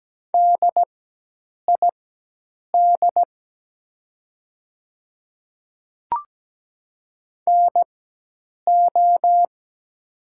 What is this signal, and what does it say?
Morse code, heard as a keyed tone, about 17 wpm, spaced as for 8 wpm, 700 Hz. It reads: DID NO
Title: 4x Character Spacing - Top 100 Words - 17wpm